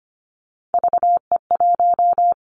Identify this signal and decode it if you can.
Morse code, heard as a keyed tone, about 25 wpm, 700 Hz. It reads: VE1
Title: Callsign Prefixes for Top 100 DXCC - 25wpm